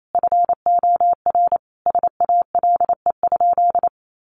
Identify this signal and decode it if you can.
Morse code, heard as a keyed tone, about 28 wpm, 700 Hz. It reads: FOR SALE?
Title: Sentences from Top 1400 Words 28wpm